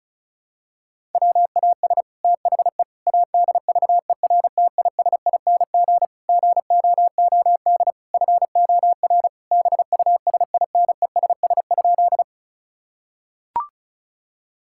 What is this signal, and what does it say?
Morse code, heard as a keyed tone, about 36 wpm, 700 Hz. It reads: WAS THE ADVERTISING GOOD FOR BUSINESS?
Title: Sentences from Top 1000 Words - Review Part 6 - 35wpm